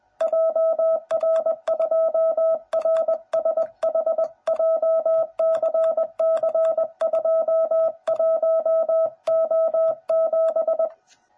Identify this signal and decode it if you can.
Morse code, heard as a keyed tone, about 20 wpm, 660 Hz. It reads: JR2RSH JCC21O7
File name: jh8cbh / 2021cw / cw09b